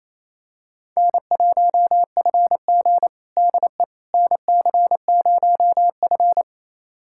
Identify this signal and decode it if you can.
Morse code, heard as a keyed tone, about 28 wpm, 700 Hz. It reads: N1FG DE NC0F
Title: USA Callsign DE Callsign - 4 Characters - 28wpm